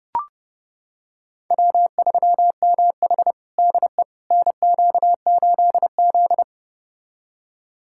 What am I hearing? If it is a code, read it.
Morse code, heard as a keyed tone, about 32 wpm, 700 Hz. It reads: W3MH DE NQ8Z
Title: USA Callsign DE Callsign - 4 Characters - 30wpm